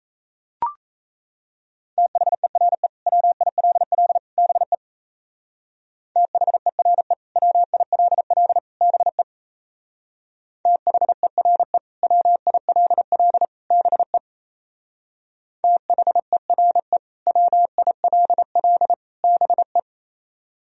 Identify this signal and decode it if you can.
Morse code, heard as a keyed tone, about 37 wpm, 700 Hz. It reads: THERE WILL BE THERE WILL BE THERE WILL BE THERE WILL BE